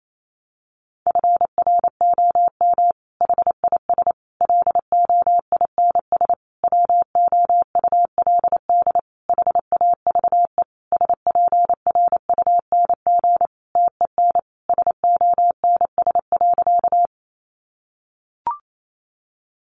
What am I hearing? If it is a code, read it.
Morse code, heard as a keyed tone, about 28 wpm, 700 Hz. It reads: FROM HIS LOINS WOULD HAVE SPRUNG TEN SONS.
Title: A Sound Of Thunder 28wpm